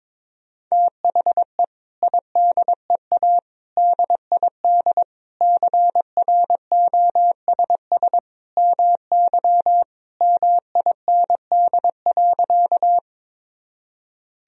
Morse code, heard as a keyed tone, about 22 wpm, 700 Hz. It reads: THE IDEA DID CROSS MY MIND.
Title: Sentences from Top 1000 Words - Review Part 1 - 22wpm